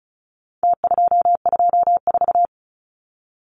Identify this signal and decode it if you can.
Morse code, heard as a keyed tone, about 34 wpm, 700 Hz. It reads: T224
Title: Numbers - Contest Style - 4-Digit 35wpm